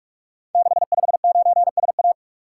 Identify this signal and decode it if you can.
Morse code, heard as a keyed tone, about 44 wpm, 700 Hz. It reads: 659SA
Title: Code Groups - 5 Letter:Numbers - 45wpm